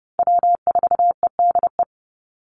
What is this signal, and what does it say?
Morse code, heard as a keyed tone, about 30 wpm, 700 Hz. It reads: W4EDE
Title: Call Signs - 30wpm